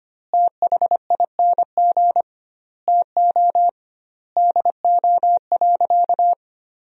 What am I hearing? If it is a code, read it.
Morse code, heard as a keyed tone, about 25 wpm, 700 Hz. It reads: THING TO DO.
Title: 2x Word-Spacing - Sentences from Top 100 Words - 25wpm